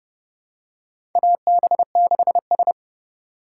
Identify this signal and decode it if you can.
Morse code, heard as a keyed tone, about 30 wpm, 700 Hz. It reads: AB6S
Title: USA Call Signs - 4 Characters - 3x Character Spacing - 30wpm